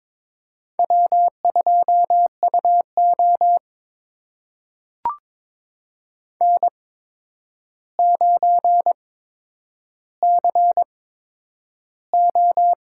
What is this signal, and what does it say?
Morse code, heard as a keyed tone, about 22 wpm, spaced as for 9 wpm, 700 Hz. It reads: W2UO N9CO
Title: USA Call Signs - 4 Characters - 8x Character Spacing - 22wpm